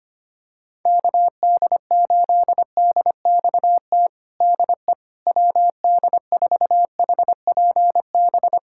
Morse code, heard as a keyed tone, about 25 wpm, 700 Hz. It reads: KD8DXT DE WD4HPB
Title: USA Callsign DE Callsign - 6 Characters - 25wpm